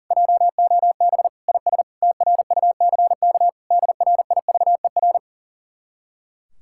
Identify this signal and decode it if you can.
Morse code, heard as a keyed tone, about 52 wpm, 700 Hz. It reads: JOB IS TRUCK DRIVER